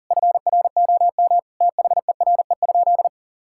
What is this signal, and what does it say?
Morse code, heard as a keyed tone, about 40 wpm, 700 Hz. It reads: FROM THERE?
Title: Sentences from Top 1000 Words - Review Part 6 - 40wpm